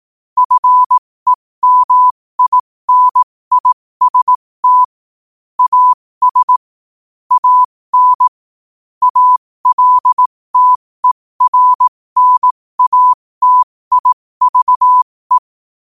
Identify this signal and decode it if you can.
Morse code, heard as a keyed tone, about 18 wpm, spaced as for 15 wpm, 1000 Hz. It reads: FEMINIST AS AN ALTERNATIVE
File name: qotdfiles / 15wpm